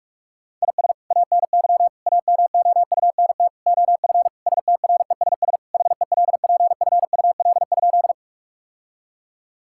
Morse code, heard as a keyed tone, about 46 wpm, 700 Hz. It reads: IS ANY AMOUNT OF STRESS HELPFUL?
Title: Sentences from Top 1000 Words - Review Part 6 - 45wpm